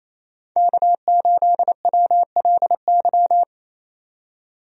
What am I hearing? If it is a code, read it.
Morse code, heard as a keyed tone, about 28 wpm, 700 Hz. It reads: K8WLY
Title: USA Call Signs - 5 Characters - 28wpm